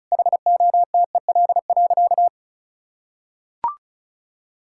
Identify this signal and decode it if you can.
Morse code, heard as a keyed tone, about 36 wpm, 700 Hz. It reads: HOTEL.
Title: Sentences from Top 1600 Words 35wpm